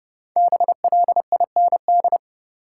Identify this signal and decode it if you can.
Morse code, heard as a keyed tone, about 30 wpm, 700 Hz. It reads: BLIND
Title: Five Letter Words - 30wpm